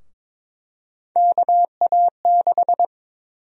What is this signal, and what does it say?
Morse code, heard as a keyed tone, about 22 wpm, 700 Hz. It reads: KA6
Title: USA Call Sign Prefixes - Repeated 2x - 22wpm